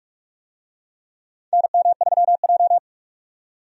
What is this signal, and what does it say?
Morse code, heard as a keyed tone, about 47 wpm, 700 Hz. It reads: NM3J
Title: USA Call Signs - 4 Characters - 45wpm